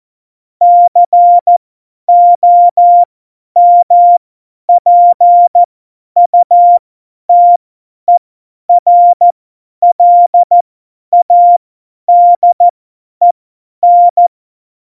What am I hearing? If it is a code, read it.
Morse code, heard as a keyed tone, about 14 wpm, spaced as for 10 wpm, 700 Hz. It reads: COMPUTERLADEN
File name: Morsekurs / Witze / Witz 15. Januar 2019, 50 F70